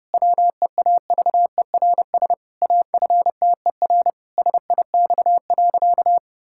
Morse code, heard as a keyed tone, about 28 wpm, 700 Hz. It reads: WEAVERS AFTER SIX.
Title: Pangram Challenge v2 - 30wpm